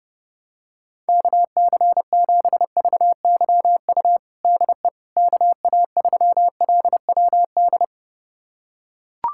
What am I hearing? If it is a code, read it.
Morse code, heard as a keyed tone, about 32 wpm, 700 Hz. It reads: KC7VYU DE KA3LWD
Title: USA Callsign DE Callsign 30wpm